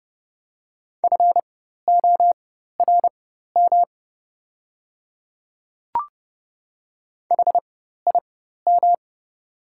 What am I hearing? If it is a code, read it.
Morse code, heard as a keyed tone, about 30 wpm, spaced as for 14 wpm, 700 Hz. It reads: FORM HIM
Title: 4x Character Spacing - Top 200 Words - 30wpm